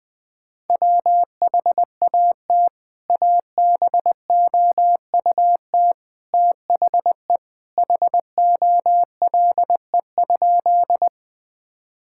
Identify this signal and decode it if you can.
Morse code, heard as a keyed tone, about 20 wpm, 700 Hz. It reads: WHAT ABOUT THE HOLE?
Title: Sentences from Top 1600 Words 20wpm